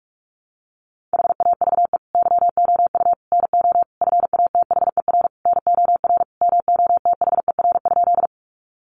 Morse code, heard as a keyed tone, about 44 wpm, 700 Hz. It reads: HAVE YOU NO FATHER NOR MOTHER?
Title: Sentences from Top 1500 Words 45wpm